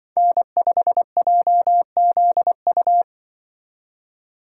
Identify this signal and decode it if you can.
Morse code, heard as a keyed tone, about 24 wpm, 700 Hz. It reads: N5JZU
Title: Speed Racing - USA 1x3 Call Signs 20wpm